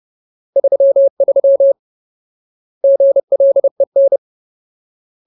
Morse code, heard as a keyed tone, about 30 wpm, 550 Hz. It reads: 33 GLEN
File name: Ultimate Word List / 30WPM Ultimate